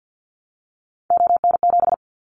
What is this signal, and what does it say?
Morse code, heard as a keyed tone, about 50 wpm, 700 Hz. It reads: ON7
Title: Callsign Prefixes for Top 25 DXCC - Repeated 3x - 50wpm